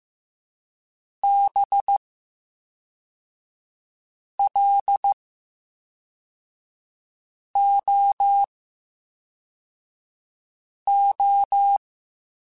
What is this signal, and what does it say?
Morse code, heard as a keyed tone, about 15 wpm, 800 Hz. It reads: B L O O
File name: Special INGs / Special INGs-mp3s / VerySlowToFaster Ing @10